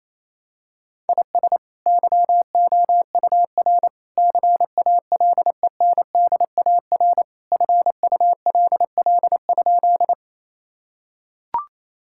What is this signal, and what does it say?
Morse code, heard as a keyed tone, about 28 wpm, 700 Hz. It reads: IS YOUR CALENDAR FULL?
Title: Sentences from Top 1600 Words 28wpm